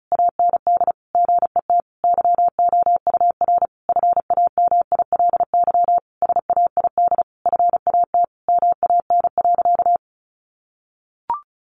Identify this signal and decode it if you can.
Morse code, heard as a keyed tone, about 34 wpm, 700 Hz. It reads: AND GET YOUR FAMILY SAID FAT MAN.
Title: Sentences from Top 1100 Words 35wpm